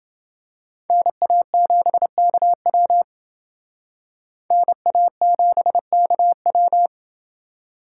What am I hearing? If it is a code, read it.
Morse code, heard as a keyed tone, about 28 wpm, 700 Hz. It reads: NA7KW NA7KW